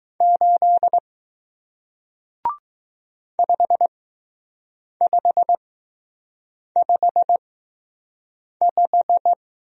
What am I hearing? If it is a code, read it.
Morse code, heard as a keyed tone, about 20 wpm, spaced as for 6 wpm, 700 Hz. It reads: 8 5555